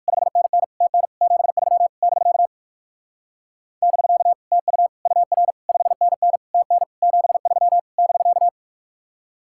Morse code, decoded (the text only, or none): 5NN TN 73 <BK> <BK> TU UR 5NN TN 73 <BK>